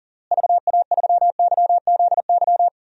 Morse code, heard as a keyed tone, about 40 wpm, 700 Hz. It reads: VA3YZY
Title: DXCC-100 Call Signs - 6 Characters - Repeated 2x - 40wpm